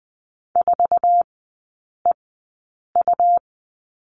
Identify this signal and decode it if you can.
Morse code, heard as a keyed tone, about 20 wpm, spaced as for 8 wpm, 700 Hz. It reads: 4EU
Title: 107 Lesson 18e - Letter m - Three Characters 20wpm